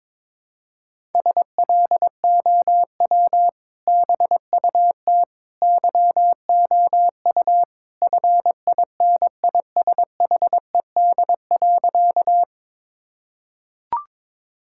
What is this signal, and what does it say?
Morse code, heard as a keyed tone, about 22 wpm, 700 Hz. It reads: SLOW BUT YOU FINISHED.
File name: practice / 22wpm / Sentences with 2-Word Phrases - 22wpm